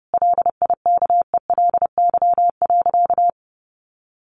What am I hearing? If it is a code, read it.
Morse code, heard as a keyed tone, about 30 wpm, 700 Hz. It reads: LIKELY.
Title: Sentences from Top 600 Words 30wpm